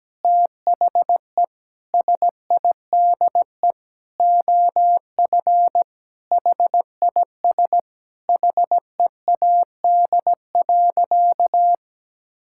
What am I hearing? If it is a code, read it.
Morse code, heard as a keyed tone, about 17 wpm, 700 Hz. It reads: THE SIDE OF HIS HEAD.
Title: Sentences from Top 1600 Words 17wpm